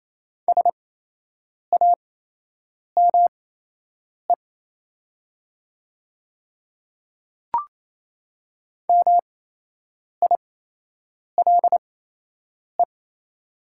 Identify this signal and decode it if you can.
Morse code, heard as a keyed tone, about 27 wpm, spaced as for 8 wpm, 700 Hz. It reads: SAME MILE
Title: 8x Character Spacing - Top 300 Words - 28wpm